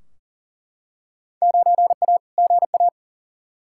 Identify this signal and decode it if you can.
Morse code, heard as a keyed tone, about 40 wpm, 700 Hz. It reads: 9A GA
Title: Field Day Exchanges Encore - 40wpm